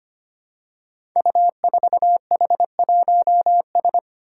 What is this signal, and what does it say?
Morse code, heard as a keyed tone, about 25 wpm, 700 Hz. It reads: U4H1S
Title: Five Challenging Letter-Number Combinations 25wpm